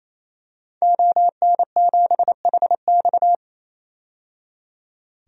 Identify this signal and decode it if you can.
Morse code, heard as a keyed tone, about 28 wpm, 700 Hz. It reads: ON7HX
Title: DXCC-100 Call Signs - 5 Characters - 28wpm